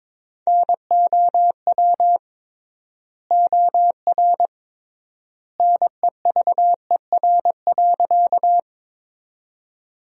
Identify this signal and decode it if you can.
Morse code, heard as a keyed tone, about 22 wpm, 700 Hz. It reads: NOW OR NEVER.
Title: 3x Word-Spacing - Sentences from Top 300 Words - 22wpm